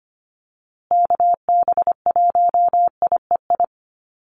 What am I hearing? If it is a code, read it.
Morse code, heard as a keyed tone, about 25 wpm, 700 Hz. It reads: KB1IEI
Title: USA Call Signs Repeated 2x 25wpm